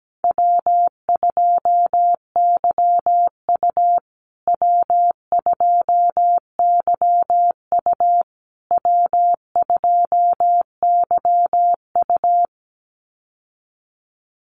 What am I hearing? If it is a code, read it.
Morse code, heard as a keyed tone, about 17 wpm, 700 Hz. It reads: W2YU W2YU W2YU
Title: USA Call Signs Repeated 3x 17wpm